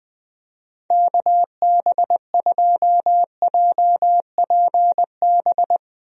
Morse code, heard as a keyed tone, about 21 wpm, 700 Hz. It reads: KB2JPB
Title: USA Call Signs - 6 Characters - Repeated 3x - 20wpm